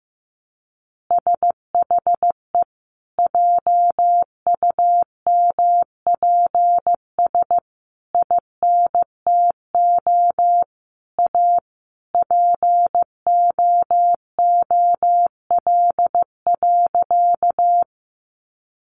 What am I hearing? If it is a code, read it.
Morse code, heard as a keyed tone, about 15 wpm, 700 Hz. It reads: SHE JUMPS INTO A POOL.
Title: Additional Easy Phrases 15wpm